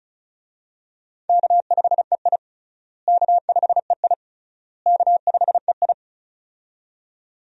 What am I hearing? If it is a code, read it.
Morse code, heard as a keyed tone, about 36 wpm, 700 Hz. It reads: K5EI K5EI K5EI
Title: USA Call Signs - 4 Characters - Repeated 3x - 35wpm